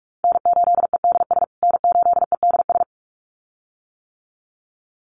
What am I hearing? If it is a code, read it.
Morse code, heard as a keyed tone, about 44 wpm, 700 Hz. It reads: N8EDS N8EDS
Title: USA Call Signs Repeated 2x 45wpm